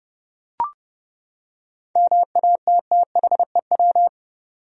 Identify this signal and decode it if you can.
Morse code, heard as a keyed tone, about 32 wpm, 700 Hz. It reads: MATTHEW
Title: USA Common Names - 30wpm